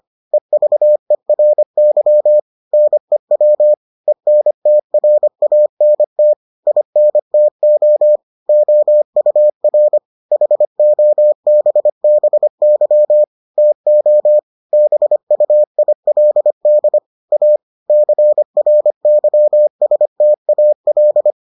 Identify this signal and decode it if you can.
Morse code, heard as a keyed tone, about 25 wpm, 600 Hz. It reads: EVERY NEW ENTRANT INTO OUR HOBBY TO BUILD A CRYSTAL